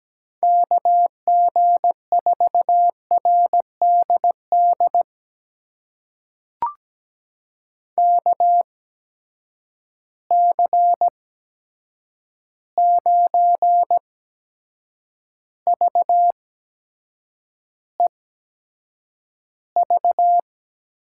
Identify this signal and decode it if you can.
Morse code, heard as a keyed tone, about 17 wpm, spaced as for 7 wpm, 700 Hz. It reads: KG4RDD KC9VEV
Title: USA Call Signs - 6 Characters - 8x Character Spacing - 17wpm